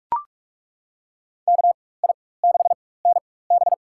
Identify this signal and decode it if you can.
Morse code, heard as a keyed tone, about 42 wpm, spaced as for 21 wpm, 700 Hz. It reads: KI6NB
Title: USA Call Signs - 5 Characters - 4x Character Spacing - 45wpm